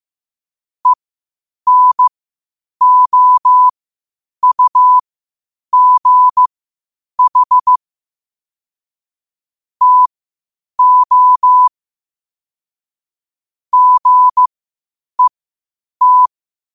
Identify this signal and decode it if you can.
Morse code, heard as a keyed tone, about 15 wpm, spaced as for 8 wpm, 1000 Hz. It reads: ENOUGH TO GET
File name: qotdfiles / 8wpm